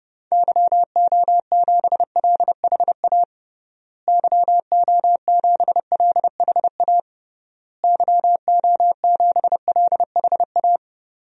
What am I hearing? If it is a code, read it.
Morse code, heard as a keyed tone, about 30 wpm, 700 Hz. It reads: YO7LHA YO7LHA YO7LHA
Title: DXCC-100 Call Signs - 6 Characters - Repeated 3x - 30wpm